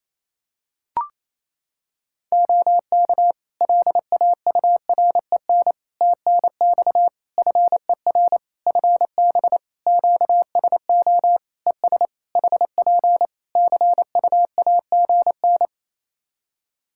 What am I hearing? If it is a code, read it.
Morse code, heard as a keyed tone, about 27 wpm, 700 Hz. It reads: OK LAUREN TNX FER FB QSO ES HP CUAGN
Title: Practice QSO - Exchange 4 - Ending - 28wpm